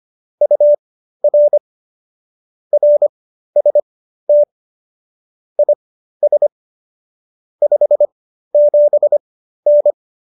Morse code, heard as a keyed tone, about 25 wpm, spaced as for 13 wpm, 590 Hz. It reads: UR RST IS 57N